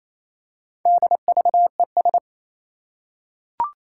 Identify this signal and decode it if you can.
Morse code, heard as a keyed tone, about 27 wpm, 700 Hz. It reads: DVES
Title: Four Challenging Letter-Number Combinations 28wpm